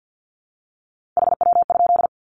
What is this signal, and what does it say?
Morse code, heard as a keyed tone, about 50 wpm, 700 Hz. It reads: HW?
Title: Common QSO Phrases 50wpm